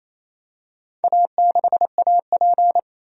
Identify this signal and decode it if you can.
Morse code, heard as a keyed tone, about 28 wpm, 700 Hz. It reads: A6AP
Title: DXCC-100 Call Signs - 4 Characters - 28wpm